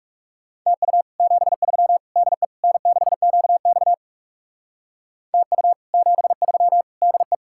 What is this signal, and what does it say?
Morse code, heard as a keyed tone, about 36 wpm, 700 Hz. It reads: TU 73 DE N6QX TU 73 DE
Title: Speed-Racing Common QSO Phrases - 30wpm